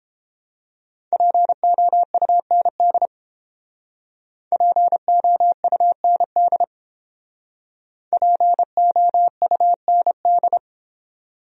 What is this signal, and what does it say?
Morse code, heard as a keyed tone, about 29 wpm, 700 Hz. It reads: POUND POUND POUND